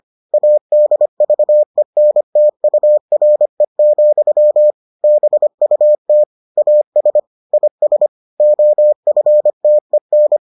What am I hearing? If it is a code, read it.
Morse code, heard as a keyed tone, about 25 wpm, 600 Hz. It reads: ADVENTURE, BUT AS IS OFTEN